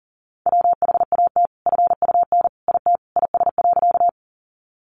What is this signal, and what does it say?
Morse code, heard as a keyed tone, about 40 wpm, 700 Hz. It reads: WHAT FUN IT IS.
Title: Sentences from Top 700 Words 40wpm